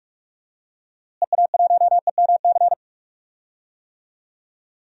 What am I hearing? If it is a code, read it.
Morse code, heard as a keyed tone, about 42 wpm, 700 Hz. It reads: EA1EMC
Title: DXCC-100 Call Signs - 6 Characters - Repeated 3x - 45wpm